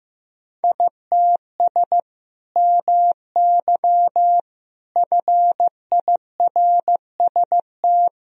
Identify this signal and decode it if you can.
Morse code, heard as a keyed tone, about 16 wpm, 700 Hz. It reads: ITS MY FIRST
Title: Sentences from Top 1000 Words - Review Part 1 - 15wpm